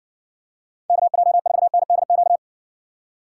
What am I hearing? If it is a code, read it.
Morse code, heard as a keyed tone, about 58 wpm, 700 Hz. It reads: KJ4NDQ